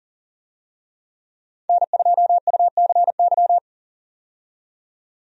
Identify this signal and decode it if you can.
Morse code, heard as a keyed tone, about 39 wpm, 700 Hz. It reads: N2UCY
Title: USA Call Signs - 5 Characters - 40wpm